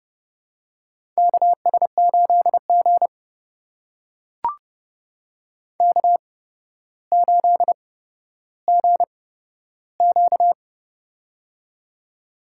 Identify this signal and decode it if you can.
Morse code, heard as a keyed tone, about 30 wpm, spaced as for 13 wpm, 700 Hz. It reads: KS8G K8GQ
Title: USA Call Signs - 4 Characters - 8x Character Spacing - 30wpm